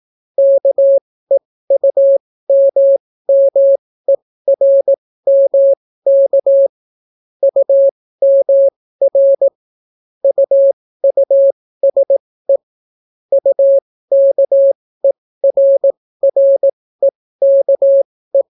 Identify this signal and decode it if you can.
Morse code, heard as a keyed tone, about 18 wpm, spaced as for 14 wpm, 550 Hz. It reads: KEUMMERMK UMR UUSE UKERREKE